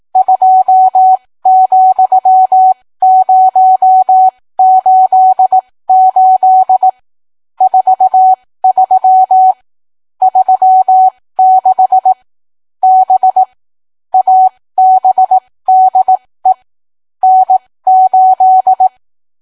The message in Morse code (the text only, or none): 2,088 43 36 B ABDE N8